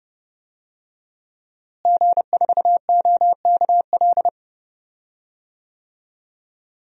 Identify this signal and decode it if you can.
Morse code, heard as a keyed tone, about 32 wpm, 700 Hz. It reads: G4OKL
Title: DXCC-100 Call Signs - 5 Characters - Repeated 3x - 30wpm